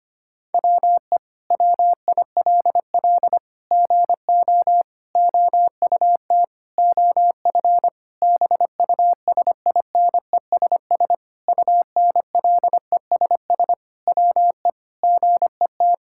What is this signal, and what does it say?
Morse code, heard as a keyed tone, about 25 wpm, 700 Hz. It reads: WE WILL GO OUT OF BUSINESS UNLESS WE GET
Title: Sentences from Top 1000 Words - Review Part 6 - 25wpm